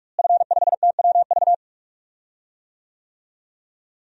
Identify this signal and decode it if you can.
Morse code, heard as a keyed tone, about 41 wpm, 700 Hz. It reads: F5TWV
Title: DXCC-100 Call Signs - 5 Characters - Repeated 3x - 45wpm